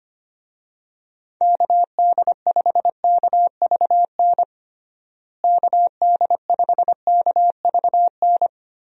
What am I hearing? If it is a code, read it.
Morse code, heard as a keyed tone, about 25 wpm, 700 Hz. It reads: KD5KVN KD5KVN
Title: USA Call Signs - 6 Characters - Repeated 3x - 25wpm